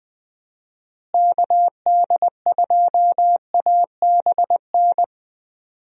Morse code, heard as a keyed tone, about 20 wpm, 700 Hz. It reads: KD2ABN
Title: USA Call Signs - 6 Characters - Repeated 2x - 20wpm